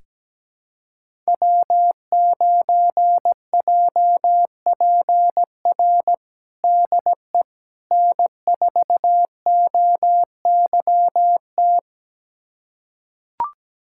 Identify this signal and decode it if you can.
Morse code, heard as a keyed tone, about 17 wpm, 700 Hz. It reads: W9JPR DE N4OYT
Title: USA Callsign DE Callsign - 5 Characters - 17wpm